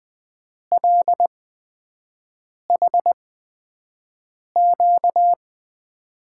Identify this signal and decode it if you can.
Morse code, heard as a keyed tone, about 20 wpm, spaced as for 5 wpm, 700 Hz. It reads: LHQ
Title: USA Call Signs - 6 Characters - 8x Character Spacing - 20wpm